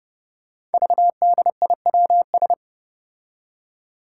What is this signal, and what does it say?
Morse code, heard as a keyed tone, about 30 wpm, 700 Hz. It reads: VDIWS
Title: Five Challenging Letter-Number Combinations 30wpm